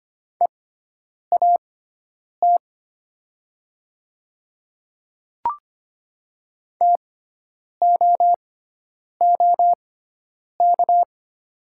Morse code, heard as a keyed tone, about 25 wpm, spaced as for 9 wpm, 700 Hz. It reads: EAT TOOK
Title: 6x Character Spacing - Top 500 Words - 25wpm